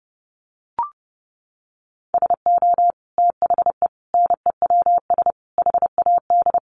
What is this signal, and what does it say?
Morse code, heard as a keyed tone, about 30 wpm, 700 Hz. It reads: SO THE NEWS HAD
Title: Sentences from Top 1200 Words 30wpm